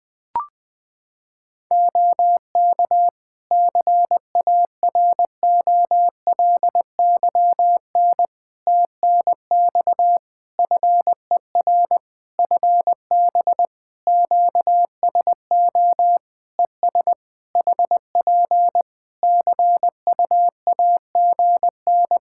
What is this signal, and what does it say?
Morse code, heard as a keyed tone, about 21 wpm, 700 Hz. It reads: OK CAROLYN TNX FER FB QSO ES HP CUAGN
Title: Practice QSO - Exchange 4 - Ending - 20wpm